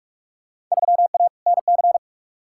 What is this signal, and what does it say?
Morse code, heard as a keyed tone, about 47 wpm, 700 Hz. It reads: 3A NC
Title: Field Day Exchanges Encore - 45wpm